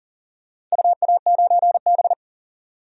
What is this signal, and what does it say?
Morse code, heard as a keyed tone, about 40 wpm, 700 Hz. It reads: UA9B
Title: DXCC-100 Call Signs - 4 Characters - 40wpm